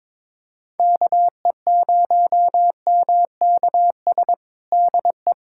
Morse code, heard as a keyed tone, about 22 wpm, 700 Hz. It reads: KE0MKS DE
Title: USA Callsign DE Callsign - 6 Characters - 22wpm